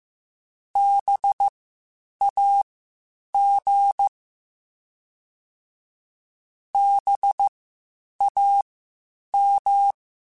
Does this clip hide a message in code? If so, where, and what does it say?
Yes: Morse code from 0.8 to 9.9 s, heard as a keyed tone, about 15 wpm, spaced as for 8 wpm, 800 Hz. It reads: BAG BAM